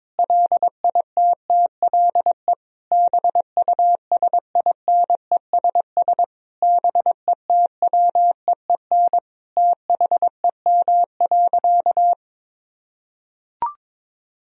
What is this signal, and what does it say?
Morse code, heard as a keyed tone, about 22 wpm, 700 Hz. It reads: LITTLE BUSINESS BETWEEN THEM.
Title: Sentences from Top 1000 Words - Review Part 1 - 22wpm